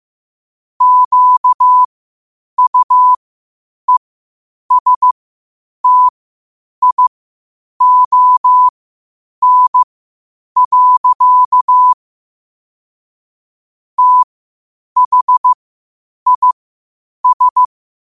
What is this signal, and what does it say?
Morse code, heard as a keyed tone, about 15 wpm, spaced as for 8 wpm, 1000 Hz. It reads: QUESTION. THIS
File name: qotdfiles / 8wpm